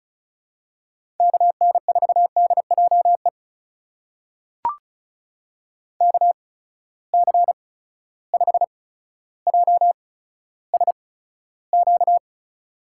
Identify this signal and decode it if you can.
Morse code, heard as a keyed tone, about 36 wpm, spaced as for 15 wpm, 700 Hz. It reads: KN4DJE KC5JSQ
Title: USA Call Signs - 6 Characters - 8x Character Spacing - 35wpm